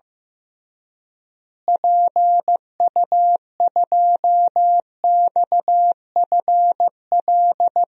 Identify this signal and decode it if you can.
Morse code, heard as a keyed tone, about 16 wpm, 700 Hz. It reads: PU2XFL
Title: DXCC-100 Call Signs - 6 Characters - Repeated 2x - 15wpm